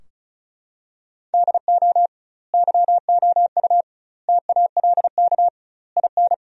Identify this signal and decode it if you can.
Morse code, heard as a keyed tone, about 36 wpm, 700 Hz. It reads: DO YOU TALK IN
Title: 2x Word-Spacing - Sentences from Top 500 Words - 35wpm